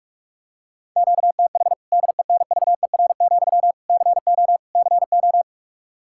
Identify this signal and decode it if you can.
Morse code, heard as a keyed tone, about 46 wpm, 700 Hz. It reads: QTH DENVER, CO CO
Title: Practice QSO - Exchange 1 - Beginning - 45wpm